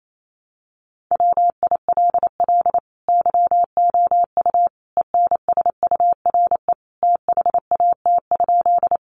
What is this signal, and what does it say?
Morse code, heard as a keyed tone, about 28 wpm, 700 Hz. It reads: WILL YOU ENSURE THAT?
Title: Sentences from Top 1100 Words 28wpm